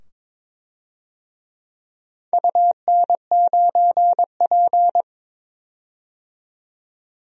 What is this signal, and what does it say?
Morse code, heard as a keyed tone, about 22 wpm, 700 Hz. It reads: UN9P
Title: DXCC-100 Call Signs - 4 Characters - 22wpm